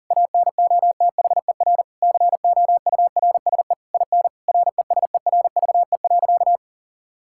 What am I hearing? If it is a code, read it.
Morse code, heard as a keyed tone, about 40 wpm, 700 Hz. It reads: ANOTHER COURSE IN RESERVE.
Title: Sentences from Top 1000 Words - Review Part 6 - 40wpm